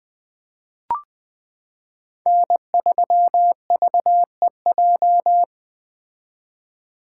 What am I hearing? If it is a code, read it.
Morse code, heard as a keyed tone, about 21 wpm, 700 Hz. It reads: N3VEJ
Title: 2023 POTA Activator Call Signs - 20wpm